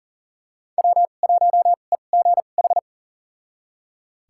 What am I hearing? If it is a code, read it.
Morse code, heard as a keyed tone, about 40 wpm, spaced as for 29 wpm, 700 Hz. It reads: W1EGH
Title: USA Call Signs - 5 Characters - 2x Character Spacing - 40wpm